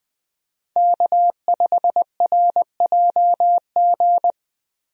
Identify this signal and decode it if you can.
Morse code, heard as a keyed tone, about 21 wpm, 700 Hz. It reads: K5RJG
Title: USA Call Signs - 5 Characters - Repeated 3x - 20wpm